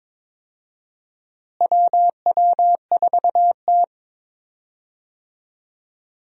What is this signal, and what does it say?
Morse code, heard as a keyed tone, about 22 wpm, 700 Hz. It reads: WW4T
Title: USA Call Signs - 4 Characters - Repeated 3x - 22wpm